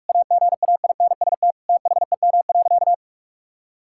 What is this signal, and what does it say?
Morse code, heard as a keyed tone, about 45 wpm, 700 Hz. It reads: AGAINST THEM.
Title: Sentences from Top 1000 Words - Review Part 6 - 45wpm